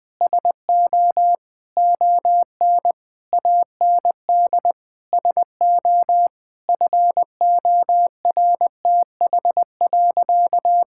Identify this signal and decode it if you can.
Morse code, heard as a keyed tone, about 20 wpm, 700 Hz. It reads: SO ON AND SO FORTH.
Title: Sentences with 6-Word Phrases - 20wpm